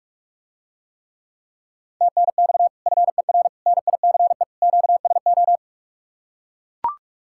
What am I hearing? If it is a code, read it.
Morse code, heard as a keyed tone, about 44 wpm, 700 Hz. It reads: TNX FER NICE QSO